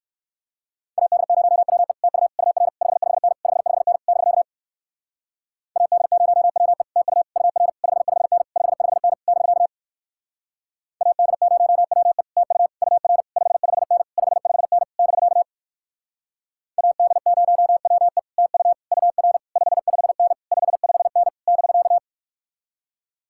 Morse code, heard as a keyed tone, about 58 wpm, 700 Hz. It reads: AD0WE TU UR 55N 55N <BK> AD0WE TU UR 55N 55N <BK> AD0WE TU UR 55N 55N <BK> AD0WE TU UR 55N 55N <BK>